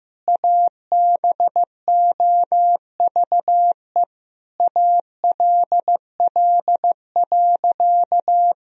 Morse code, heard as a keyed tone, about 15 wpm, 700 Hz. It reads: ABOVE ALL.
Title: Sentences from Top 1000 Words - Review Part 2 - 15wpm